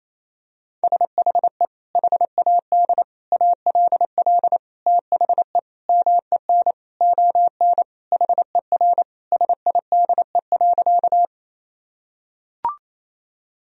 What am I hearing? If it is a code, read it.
Morse code, heard as a keyed tone, about 28 wpm, 700 Hz. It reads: SHE HAD ALL THE MEN ON HER SIDE.
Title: Sentences from Top 1000 Words - Review Part 1 - 28wpm